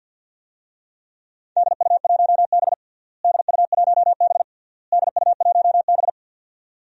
Morse code, heard as a keyed tone, about 50 wpm, 700 Hz. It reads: DU1B DU1B DU1B
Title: DXCC-100 Call Signs - 4 Characters - Repeated 3x - 50wpm